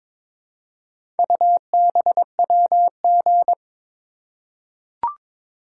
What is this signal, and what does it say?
Morse code, heard as a keyed tone, about 22 wpm, 700 Hz. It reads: UBWG
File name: practice / 22wpm / Four Letter-Number Combinations - 22wpm